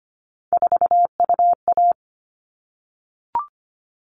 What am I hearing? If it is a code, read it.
Morse code, heard as a keyed tone, about 25 wpm, 700 Hz. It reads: 4UA
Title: All Cut Numbers - 3-Digit 25wpm